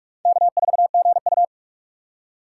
Morse code, heard as a keyed tone, about 67 wpm, spaced as for 56 wpm, 700 Hz. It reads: K4GU
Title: Speed Racing - USA Call Signs 45wpm